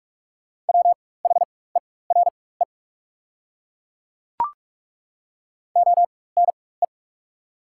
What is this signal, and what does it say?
Morse code, heard as a keyed tone, about 43 wpm, spaced as for 21 wpm, 690 Hz. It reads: WHERE ONE
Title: 4x Character Spacing - Top 200 Words - 45wpm